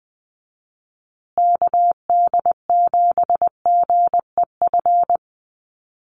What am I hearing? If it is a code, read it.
Morse code, heard as a keyed tone, about 20 wpm, 700 Hz. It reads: KD7GEF
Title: USA Call Signs Repeated 3x 20wpm